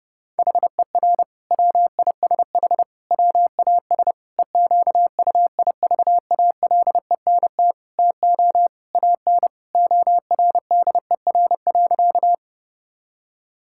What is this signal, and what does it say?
Morse code, heard as a keyed tone, about 30 wpm, 700 Hz. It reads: HER WISH WAS EQUIVALENT TO AN ORDER.
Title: Sentences from Top 1600 Words 30wpm